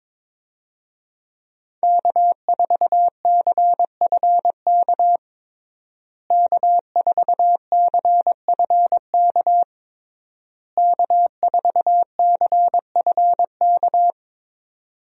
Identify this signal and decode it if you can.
Morse code, heard as a keyed tone, about 22 wpm, 700 Hz. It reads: K4CFK K4CFK K4CFK
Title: USA Call Signs - 5 Characters - Repeated 3x - 22wpm